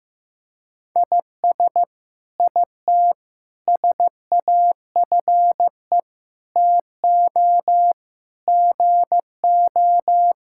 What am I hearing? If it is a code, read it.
Morse code, heard as a keyed tone, about 15 wpm, 700 Hz. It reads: IS IT SAFE TO GO
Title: Sentences from Top 1000 Words - Review Part 6 - 15wpm